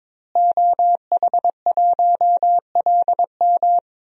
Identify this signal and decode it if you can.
Morse code, heard as a keyed tone, about 22 wpm, 700 Hz. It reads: OH1LM
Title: DXCC-100 Call Signs - 5 Characters - Repeated 2x - 22wpm